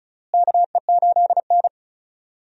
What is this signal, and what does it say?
Morse code, heard as a keyed tone, about 36 wpm, 700 Hz. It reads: KE8N
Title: USA Call Signs - 4 Characters - Repeated 3x - 35wpm